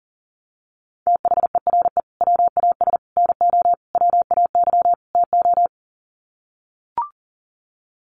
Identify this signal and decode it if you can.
Morse code, heard as a keyed tone, about 40 wpm, 700 Hz. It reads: THERE WAS NO WAY TO
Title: Sets of 5 Words Encore 40wpm